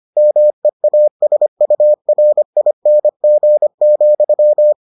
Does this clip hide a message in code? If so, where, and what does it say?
Yes: Morse code from 0.2 to 4.7 s, heard as a keyed tone, about 25 wpm, 600 Hz. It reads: MEASURING,